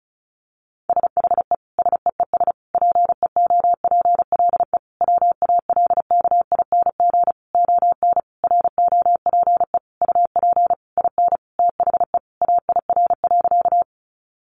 Morse code, heard as a keyed tone, about 34 wpm, 700 Hz. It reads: SHE SEES PEOPLE WALKING ON ROPE UP IN THE AIR.
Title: Easy Phrases 35wpm